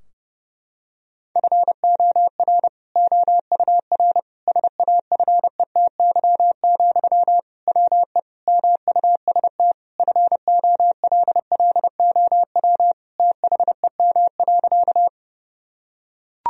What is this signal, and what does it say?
Morse code, heard as a keyed tone, about 30 wpm, 700 Hz. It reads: FOR OUR SAFETY, WE MUST FOLLOW THEM.
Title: Sentences from Top 1000 Words - Review Part 4 - 30wpm